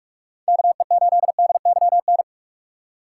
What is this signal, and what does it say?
Morse code, heard as a keyed tone, about 46 wpm, 700 Hz. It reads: KE8DYN
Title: USA Call Signs - 6 Characters - 2x Character Spacing - 45wpm